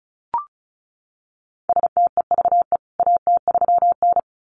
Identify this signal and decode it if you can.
Morse code, heard as a keyed tone, about 34 wpm, 700 Hz. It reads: STEVE AT3N
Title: CWT Exchange 35wpm